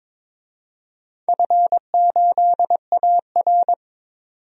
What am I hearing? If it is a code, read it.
Morse code, heard as a keyed tone, about 21 wpm, 700 Hz. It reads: F8AR
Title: DXCC-100 Call Signs - 4 Characters - 22wpm